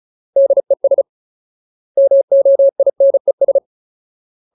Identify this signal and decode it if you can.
Morse code, heard as a keyed tone, about 34 wpm, 550 Hz. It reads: DES MOINES
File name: Ultimate Word List / 35WPM Ultimate